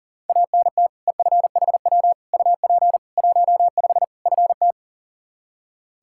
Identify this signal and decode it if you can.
Morse code, heard as a keyed tone, about 40 wpm, 700 Hz. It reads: ANT EFHW UP 15 FT
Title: Practice QSO - Exchange 2 - Info - 40wpm